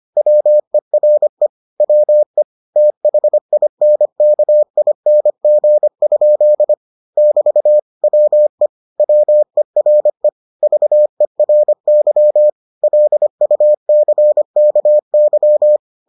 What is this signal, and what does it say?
Morse code, heard as a keyed tone, about 25 wpm, 600 Hz. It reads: WERE WE THINKING? <BT> WE WERE VERY LUCKY